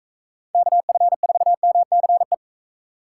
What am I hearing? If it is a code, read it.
Morse code, heard as a keyed tone, about 45 wpm, 700 Hz. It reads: KF4MCE